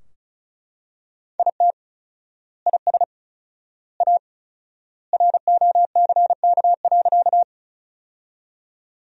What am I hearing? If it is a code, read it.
Morse code, heard as a keyed tone, about 36 wpm, 700 Hz. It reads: IT IS A ROCK.
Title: 4x Word-Spacing - Sentences from Top 400 Words - 35wpm